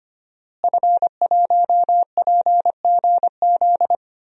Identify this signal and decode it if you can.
Morse code, heard as a keyed tone, about 25 wpm, 700 Hz. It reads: F1PGZ
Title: DXCC-100 Call Signs - 5 Characters - Repeated 2x - 25wpm